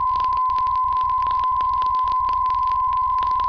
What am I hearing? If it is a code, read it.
PSK63, about 62.5 baud.
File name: UVB33-2016-06-11-1500utc-12025khz-byW2IBC